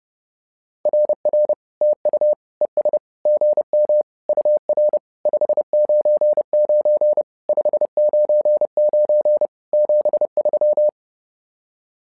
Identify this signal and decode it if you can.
Morse code, heard as a keyed tone, about 30 wpm, 600 Hz. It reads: RR TU ES GM UR 599 599 73
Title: SOTA QSO Practice 30wpm